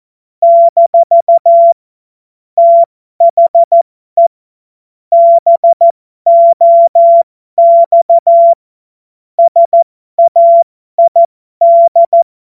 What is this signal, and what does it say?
Morse code, heard as a keyed tone, about 14 wpm, spaced as for 12 wpm, 680 Hz. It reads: - THE BOX SAID
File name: Morsekurs / Witze